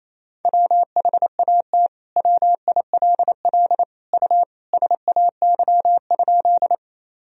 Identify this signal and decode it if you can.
Morse code, heard as a keyed tone, about 28 wpm, 700 Hz. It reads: WHAT WILL U SAY?
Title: Sentences from Top 200 Words and Abbreviations 28wpm